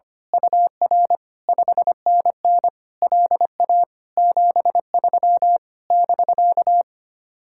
Morse code, heard as a keyed tone, about 24 wpm, 700 Hz. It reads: UR 5NN LA 73 <BK>
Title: Speed-Racing Common QSO Phrases - 25wpm